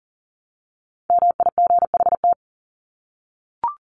Morse code, heard as a keyed tone, about 40 wpm, 700 Hz. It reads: MIGHT
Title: Top 300 Words 40wpm